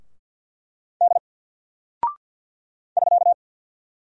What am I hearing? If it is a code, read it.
Morse code, heard as a keyed tone, about 50 wpm, 700 Hz. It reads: D <SK>
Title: Single Character (Letter:Number:Punctuation:Prosign) 50wpm